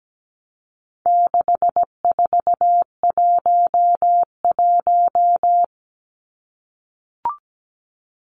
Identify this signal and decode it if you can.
Morse code, heard as a keyed tone, about 17 wpm, 700 Hz. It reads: 6411
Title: Numbers - 4-Digit 17wpm